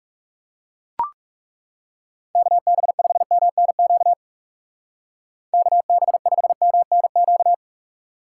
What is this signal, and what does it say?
Morse code, heard as a keyed tone, about 36 wpm, 700 Hz. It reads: KB5MNQ KB5MNQ
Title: Speed Racing - USA 2x3 Call Signs 30wpm